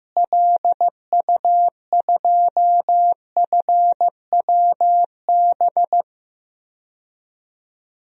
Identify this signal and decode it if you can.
Morse code, heard as a keyed tone, about 15 wpm, 700 Hz. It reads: LU2FWB
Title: DXCC-100 Call Signs - 6 Characters - Repeated 2x - 15wpm